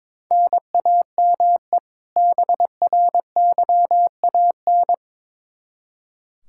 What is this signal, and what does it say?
Morse code, heard as a keyed tone, about 22 wpm, 700 Hz. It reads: NAME BRYAN
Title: K1USN SST QSO Practice 22wpm